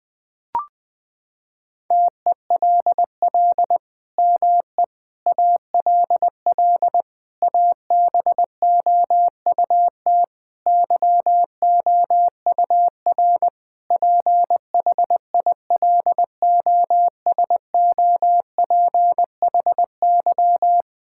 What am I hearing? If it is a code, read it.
Morse code, heard as a keyed tone, about 19 wpm, 700 Hz. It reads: TELL ME ALL ABOUT YOUR PHILOSOPHY
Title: Sentences from Top 1000 Words - Review Part 6 - 20wpm